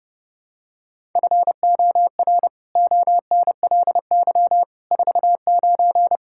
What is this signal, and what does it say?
Morse code, heard as a keyed tone, about 30 wpm, 700 Hz. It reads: FOR ONLY 49
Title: Pangram Challenge v2 - 30wpm